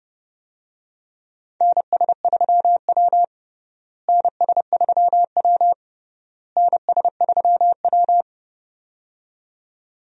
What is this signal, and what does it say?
Morse code, heard as a keyed tone, about 30 wpm, 700 Hz. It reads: NS3W NS3W NS3W
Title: USA Call Signs - 4 Characters - Repeated 3x - 30wpm